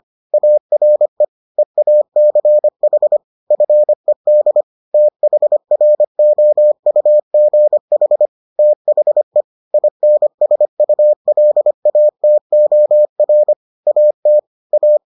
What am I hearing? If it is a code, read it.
Morse code, heard as a keyed tone, about 25 wpm, 600 Hz. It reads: ARE EACH FED THROUGH THE INSULATOR AT A